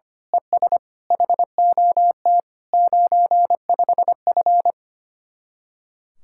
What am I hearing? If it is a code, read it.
Morse code, heard as a keyed tone, about 25 wpm, 700 Hz. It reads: ES HOT 95F
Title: Speed-Racing Common QSO Phrases Encore - 25wpm